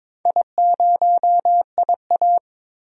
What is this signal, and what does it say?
Morse code, heard as a keyed tone, about 22 wpm, 700 Hz. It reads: I0IA
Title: DXCC-100 Call Signs - 4 Characters - 22wpm